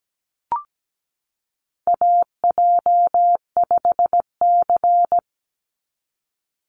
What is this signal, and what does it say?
Morse code, heard as a keyed tone, about 17 wpm, 700 Hz. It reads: AJ5C
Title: NA SOTA Chaser Callsigns 17wpm